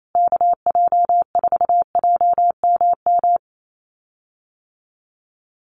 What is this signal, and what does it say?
Morse code, heard as a keyed tone, about 28 wpm, 700 Hz. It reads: KJ4JMM
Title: USA Call Signs Repeated 3x 28wpm